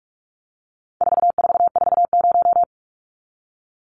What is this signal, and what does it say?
Morse code, heard as a keyed tone, about 44 wpm, 700 Hz. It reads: 4440
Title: Numbers - 4-Digit 45wpm